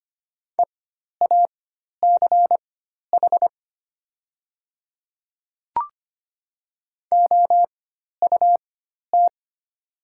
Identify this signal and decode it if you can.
Morse code, heard as a keyed tone, about 25 wpm, spaced as for 12 wpm, 700 Hz. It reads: EACH OUT
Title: 4x Character Spacing - Top 100 Words - 25wpm